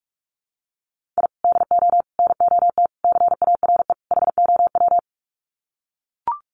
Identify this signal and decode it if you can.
Morse code, heard as a keyed tone, about 44 wpm, 700 Hz. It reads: I DO NOT CARE HOW
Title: Sets of 5 Words Encore2 45wpm